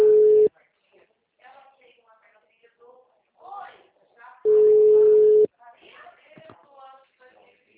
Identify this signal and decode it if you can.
Ringback tone (Europe, 425 Hz), with speech.